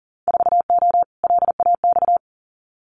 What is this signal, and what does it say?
Morse code, heard as a keyed tone, about 40 wpm, 700 Hz. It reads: 4O LAX
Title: Winter Field Day Exchanges Encore 40wpm